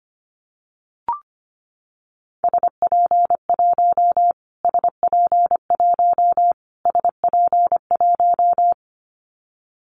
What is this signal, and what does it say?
Morse code, heard as a keyed tone, about 25 wpm, 700 Hz. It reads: SP1 SP1 SP1
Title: Callsign Prefixes for Top 25 DXCC - Repeated 3x - 25wpm